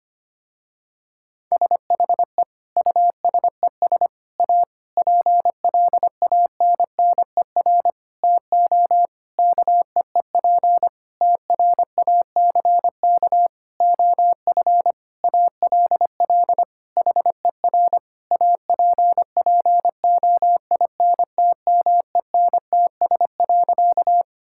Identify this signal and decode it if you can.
Morse code, heard as a keyed tone, about 25 wpm, 700 Hz. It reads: SHE USES A PLANNER TO KEEP TRACK OF ALL HER APPOINTMENTS.
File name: practice / 25wpm / Difficult Sentences with Hints - 25wpm